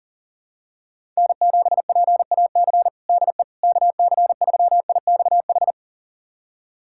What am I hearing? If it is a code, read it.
Morse code, heard as a keyed tone, about 40 wpm, 700 Hz. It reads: N7PAC DE KC3IXH
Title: USA Callsign DE Callsign 40wpm